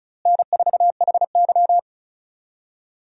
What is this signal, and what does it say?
Morse code, heard as a keyed tone, about 36 wpm, 700 Hz. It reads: N4HY
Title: USA Call Signs - 4 Characters - Repeated 2x - 35wpm